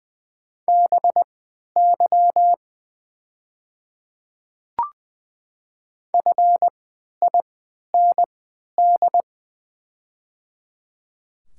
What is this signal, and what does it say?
Morse code, heard as a keyed tone, about 20 wpm, spaced as for 11 wpm, 700 Hz. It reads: BY FIND
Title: 3x Character Spacing - Top 300 Words - 20wpm